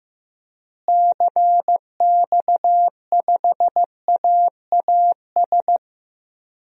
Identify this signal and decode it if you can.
Morse code, heard as a keyed tone, about 15 wpm, 700 Hz. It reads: CX5AAS
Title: DXCC-100 Call Signs - 6 Characters - Repeated 3x - 15wpm